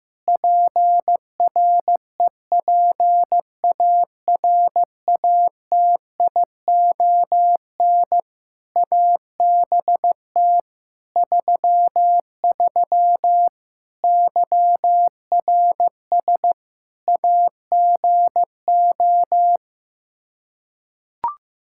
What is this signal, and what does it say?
Morse code, heard as a keyed tone, about 15 wpm, 700 Hz. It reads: PREPARATION ABT 33 YRS AGO
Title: K1USN SST QSO Practice 15wpm